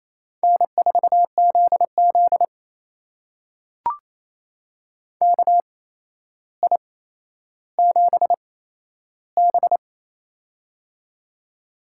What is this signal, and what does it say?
Morse code, heard as a keyed tone, about 28 wpm, spaced as for 12 wpm, 700 Hz. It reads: N4ZZ KI7B
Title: USA Call Signs - 4 Characters - 8x Character Spacing - 28wpm